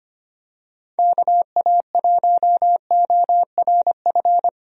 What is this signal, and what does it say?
Morse code, heard as a keyed tone, about 25 wpm, 700 Hz. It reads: KA1ORF
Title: USA Call Signs - 6 Characters - Repeated 3x - 25wpm